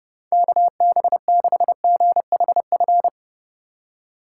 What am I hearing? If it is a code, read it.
Morse code, heard as a keyed tone, about 27 wpm, 700 Hz. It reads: KB6GHF